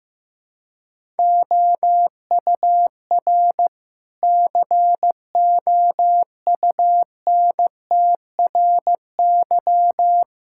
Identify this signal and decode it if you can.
Morse code, heard as a keyed tone, about 15 wpm, 700 Hz. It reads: OUR COUNTRY
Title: Sentences from Top 1000 Words - Review Part 6 - 15wpm